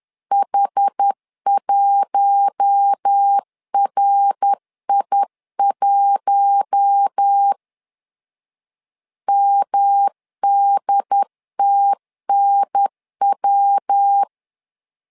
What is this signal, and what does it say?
Morse code, heard as a keyed tone, about 10 wpm, 800 Hz. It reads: H1RI1 MDTNW
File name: LEZIONI / 4 Q1 10wpm 2448